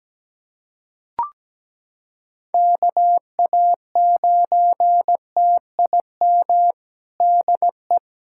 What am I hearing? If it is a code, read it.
Morse code, heard as a keyed tone, about 17 wpm, 700 Hz. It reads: KA9TIM DE
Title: USA Callsign DE Callsign - 6 Characters - 17wpm